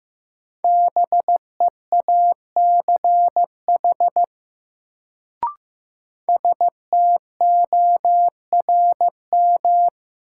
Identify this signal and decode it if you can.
Morse code, heard as a keyed tone, about 15 wpm, 700 Hz. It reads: BEACH STORM
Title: Sentences from Top 1500 Words - New Word Drill - 15wpm